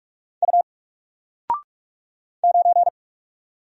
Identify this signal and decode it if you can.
Morse code, heard as a keyed tone, about 44 wpm, 700 Hz. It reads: U 9
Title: Single Character (Letter:Number:Punctuation:Prosign) Alt 45wpm